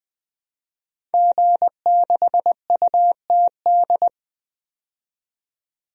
Morse code, heard as a keyed tone, about 20 wpm, 700 Hz. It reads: G6UTD
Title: DXCC-100 Call Signs - 5 Characters - 20wpm